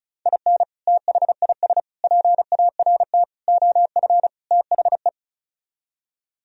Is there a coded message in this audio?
Yes: Morse code from 0.3 to 5.1 s, heard as a keyed tone, about 44 wpm, 700 Hz. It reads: IN THIS PART OF THE